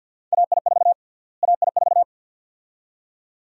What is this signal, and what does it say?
Morse code, heard as a keyed tone, about 51 wpm, 700 Hz. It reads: AI4 AI4
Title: USA Call Sign Prefixes - Repeated 2x - 50wpm